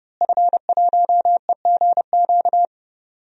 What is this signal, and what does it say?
Morse code, heard as a keyed tone, about 30 wpm, 700 Hz. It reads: F1EGQ
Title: DXCC-100 Call Signs - 5 Characters - Repeated 2x - 30wpm